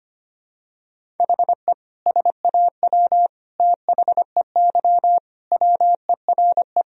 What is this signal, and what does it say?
Morse code, heard as a keyed tone, about 25 wpm, 700 Hz. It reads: HE SAW THEY WERE
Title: Sentences from Top 1000 Words - Review Part 4 - 25wpm